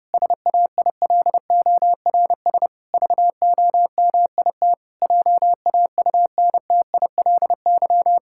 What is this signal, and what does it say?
Morse code, heard as a keyed tone, about 30 wpm, 700 Hz. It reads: SAILORS VOMIT JAUNTILY
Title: Pangram Challenge v2 - 30wpm